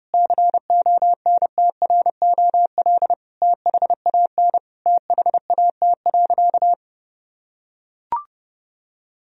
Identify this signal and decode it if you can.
Morse code, heard as a keyed tone, about 30 wpm, 700 Hz. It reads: CONTROL THAN THAT.
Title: Sentences from Top 1000 Words - Review Part 3 - 30wpm